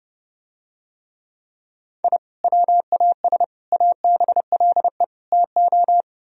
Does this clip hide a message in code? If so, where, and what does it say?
Yes: Morse code from 2.0 to 6.0 s, heard as a keyed tone, about 32 wpm, 700 Hz. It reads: I WAS ABLE TO